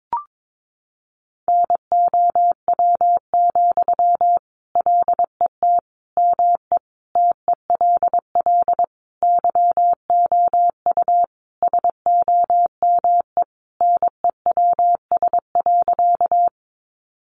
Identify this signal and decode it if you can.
Morse code, heard as a keyed tone, about 22 wpm, 700 Hz. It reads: NOW, LET ME TELL YOU SOME NEWS.
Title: Sentences from Top 800 Words 22wpm